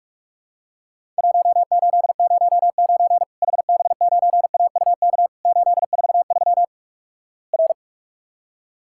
Morse code, heard as a keyed tone, about 46 wpm, 700 Hz. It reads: 1809 HB9AUK 843 R
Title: Worked All Europe DX Contest - QTC Exchange - 45wpm